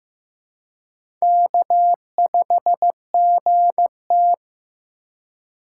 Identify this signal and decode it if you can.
Morse code, heard as a keyed tone, about 15 wpm, 700 Hz. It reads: K5GT
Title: USA Call Signs - 4 Characters - 4x Character Spacing - 15wpm